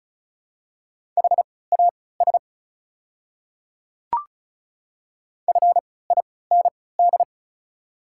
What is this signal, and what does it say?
Morse code, heard as a keyed tone, about 34 wpm, spaced as for 20 wpm, 690 Hz. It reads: HAS FIND
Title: 3x Character Spacing - Top 100 Words - 35wpm